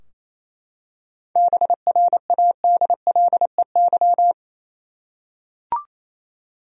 Morse code, heard as a keyed tone, about 28 wpm, 700 Hz. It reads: BRADLEY
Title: USA Common Names - 28wpm